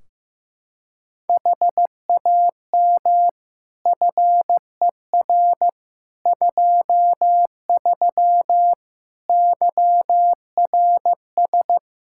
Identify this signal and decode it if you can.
Morse code, heard as a keyed tone, about 16 wpm, 700 Hz. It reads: HAM FER 23 YRS
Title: Practice QSO - Exchange 3 - More Info - 15wpm